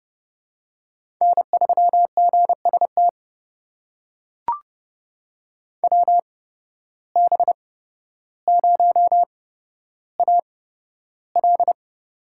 Morse code, heard as a keyed tone, about 31 wpm, spaced as for 13 wpm, 700 Hz. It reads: N3GST WB0AL
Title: USA Call Signs - 5 Characters - 8x Character Spacing - 30wpm